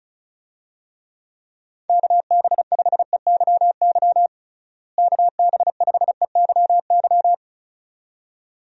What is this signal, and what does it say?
Morse code, heard as a keyed tone, about 36 wpm, 700 Hz. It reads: KB5EYY KB5EYY
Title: USA Call Signs - 6 Characters - Repeated 2x - 35wpm